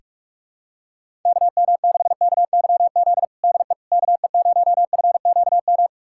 Morse code, heard as a keyed tone, about 43 wpm, 700 Hz. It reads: KM6KYZ DE KE0FQM
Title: USA Callsign DE Callsign 45wpm